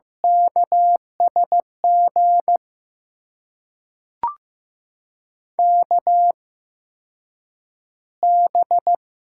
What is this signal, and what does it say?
Morse code, heard as a keyed tone, about 15 wpm, spaced as for 8 wpm, 700 Hz. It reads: KSG KB
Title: USA Call Signs - 6 Characters - 8x Character Spacing - 15wpm